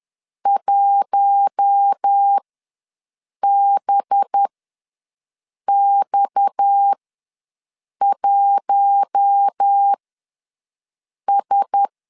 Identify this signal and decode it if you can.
Morse code, heard as a keyed tone, about 10 wpm, spaced as for 5 wpm, 800 Hz. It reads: 1BX1S
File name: LEZIONI / 7 GF9Z / 7 G ws 10wpm 3843